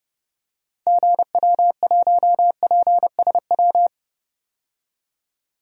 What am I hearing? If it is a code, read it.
Morse code, heard as a keyed tone, about 30 wpm, 700 Hz. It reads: GW1PSW
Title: DXCC-100 Call Signs - 6 Characters - 30wpm